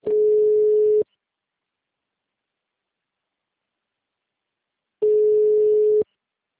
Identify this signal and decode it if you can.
Ringback tone (Europe, 425 Hz), with speech.